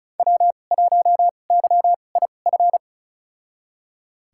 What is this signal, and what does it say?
Morse code, heard as a keyed tone, about 36 wpm, spaced as for 25 wpm, 700 Hz. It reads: W1YIF
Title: USA Call Signs - 5 Characters - 2x Character Spacing - 35wpm